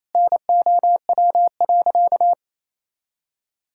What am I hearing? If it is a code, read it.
Morse code, heard as a keyed tone, about 28 wpm, 700 Hz. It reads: NOW.
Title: Sentences from Top 1000 Words - Review Part 6 - 28wpm